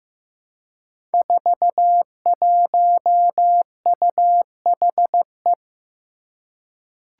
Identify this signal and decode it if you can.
Morse code, heard as a keyed tone, about 15 wpm, 700 Hz. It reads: 41UHE
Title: Five Challenging Letter-Number Combinations 15wpm